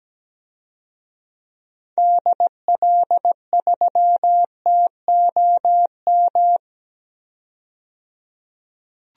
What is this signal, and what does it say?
Morse code, heard as a keyed tone, about 17 wpm, 700 Hz. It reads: DL3TOM
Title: DXCC-100 Call Signs - 6 Characters - 17wpm